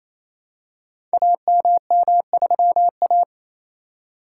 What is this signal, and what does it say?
Morse code, heard as a keyed tone, about 28 wpm, 700 Hz. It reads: AMM3A
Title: Code Groups - 5 Letter:Numbers - 28wpm